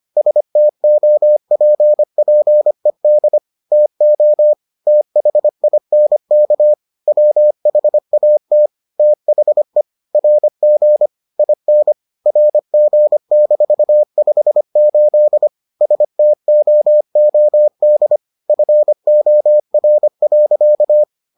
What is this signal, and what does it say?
Morse code, heard as a keyed tone, about 25 wpm, 600 Hz. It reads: STOPPED TO THINK WHAT THE RG IN RG-58 STOOD FOR.